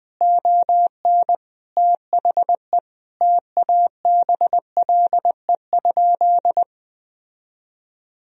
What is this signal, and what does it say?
Morse code, heard as a keyed tone, about 21 wpm, 700 Hz. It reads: ON THE TABLE?
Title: Sentences from Top 1000 Words - Review Part 2 - 20wpm